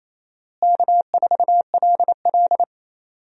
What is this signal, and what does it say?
Morse code, heard as a keyed tone, about 27 wpm, 700 Hz. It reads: K4LL
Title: USA Call Signs - 4 Characters - Repeated 3x - 28wpm